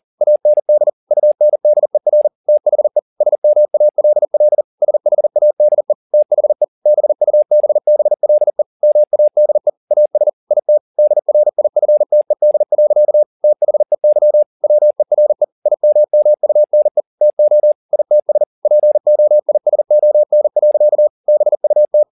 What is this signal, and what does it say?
Morse code, heard as a keyed tone, about 40 wpm, 600 Hz. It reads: AND UNDER THE SMALL SHADE THE BUBBLE MADE AS IT DRIFTED. THEY WERE IMMUNE TO ITS POISON. BUT